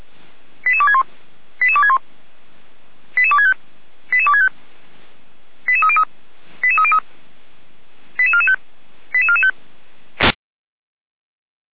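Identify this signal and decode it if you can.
Five-tone selective call (ZVEI), heard as quick runs of five tones, tones of about 70 ms. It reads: 88261 88261 88265 88265 88303 88303 88445 88445
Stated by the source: ZVEI five-tone selective call 88247 received by SDR, logged by feuerwehr-orlamuende fire brigade.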